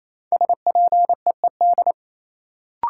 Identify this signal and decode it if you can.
Morse code, heard as a keyed tone, about 28 wpm, 700 Hz. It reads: SPEED
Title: Sentences from Top 800 Words - New Word Drill - 28wpm